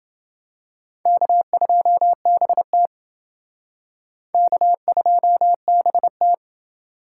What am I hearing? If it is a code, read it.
Morse code, heard as a keyed tone, about 29 wpm, 700 Hz. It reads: K2BT K2BT